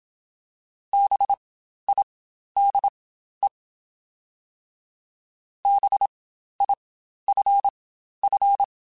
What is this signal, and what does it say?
Morse code, heard as a keyed tone, about 27 wpm, spaced as for 12 wpm, 800 Hz. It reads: BIDE BIFF
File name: Spaced Alphas / Spaced Alphas mp3s / Bs sp @25